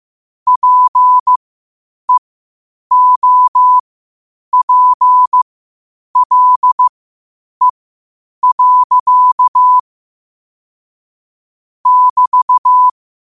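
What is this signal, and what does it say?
Morse code, heard as a keyed tone, about 15 wpm, spaced as for 8 wpm, 1000 Hz. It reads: PEOPLE. <BT>
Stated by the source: Morse code quotes at 8 words per minute from QOTD at 8 WPM.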